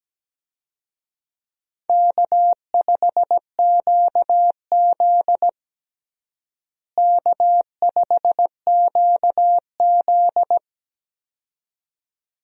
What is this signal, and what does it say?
Morse code, heard as a keyed tone, about 17 wpm, 700 Hz. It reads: K5QZ K5QZ
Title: USA Call Signs - 4 Characters - Repeated 2x - 17wpm